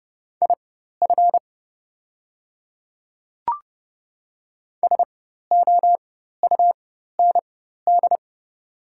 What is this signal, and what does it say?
Morse code, heard as a keyed tone, about 29 wpm, spaced as for 14 wpm, 700 Hz. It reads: IF SOUND
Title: 4x Character Spacing - Top 100 Words - 30wpm